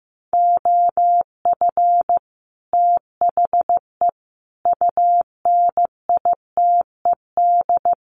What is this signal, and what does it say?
Morse code, heard as a keyed tone, about 15 wpm, 700 Hz. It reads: OF THE UNITED
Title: Sets of 4 Words 15wpm